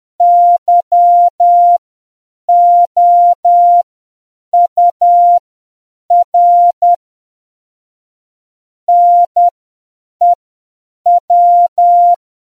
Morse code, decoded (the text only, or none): YOUR NEW